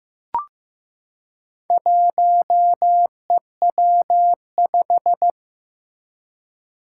Morse code, heard as a keyed tone, about 15 wpm, 700 Hz. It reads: 1EW5
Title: Four Challenging Letter-Number Combinations 15wpm